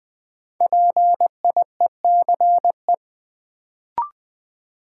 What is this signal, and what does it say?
Morse code, heard as a keyed tone, about 20 wpm, 700 Hz. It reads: PIECE
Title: Sentences from Top 400 Words - New Word Drill - 20wpm